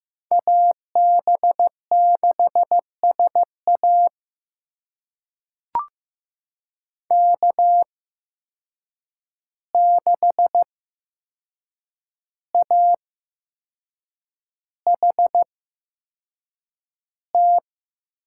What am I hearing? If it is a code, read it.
Morse code, heard as a keyed tone, about 15 wpm, spaced as for 6 wpm, 700 Hz. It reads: AB6SA K6AHT
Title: USA Call Signs - 5 Characters - 8x Character Spacing - 15wpm